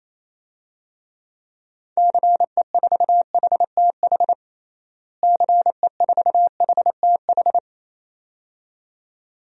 Morse code, heard as a keyed tone, about 28 wpm, 700 Hz. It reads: CE4HTH CE4HTH
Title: DXCC-100 Call Signs - 6 Characters - Repeated 2x - 28wpm